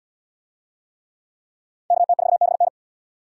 Morse code, heard as a keyed tone, about 74 wpm, 700 Hz. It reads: KT4BN